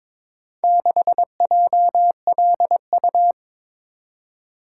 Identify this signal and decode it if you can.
Morse code, heard as a keyed tone, about 22 wpm, 700 Hz. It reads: 6JLU
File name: practice / 22wpm / Four Letter-Number Combinations - 22wpm